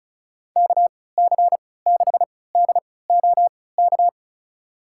Morse code, heard as a keyed tone, about 34 wpm, spaced as for 20 wpm, 700 Hz. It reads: KC6DOK
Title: USA Call Signs - 6 Characters - 3x Character Spacing - 35wpm